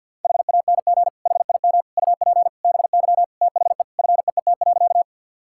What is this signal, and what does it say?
Morse code, heard as a keyed tone, about 50 wpm, 700 Hz. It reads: HANG HIM UP BY THE FEET.
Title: Sentences from Top 1600 Words 50wpm